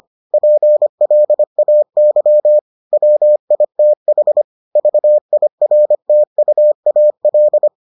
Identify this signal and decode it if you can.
Morse code, heard as a keyed tone, about 25 wpm, 600 Hz. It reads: PLAY WITH VIRTUAL